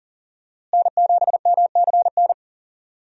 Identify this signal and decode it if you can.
Morse code, heard as a keyed tone, about 40 wpm, 700 Hz. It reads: N7MCN
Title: USA Call Signs - 5 Characters - Repeated 2x - 40wpm